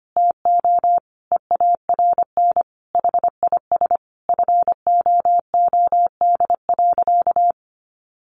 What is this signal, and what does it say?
Morse code, heard as a keyed tone, about 25 wpm, 700 Hz. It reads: TO EARN HIS FOOD.
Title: Sentences from Top 1500 Words 25wpm